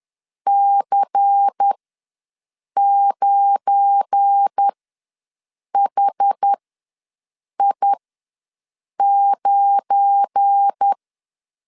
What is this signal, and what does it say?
Morse code, heard as a keyed tone, about 10 wpm, spaced as for 6 wpm, 800 Hz. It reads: C9HI9
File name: LEZIONI / 7 GF9Z / 7 GF9 ws 10wpm 4034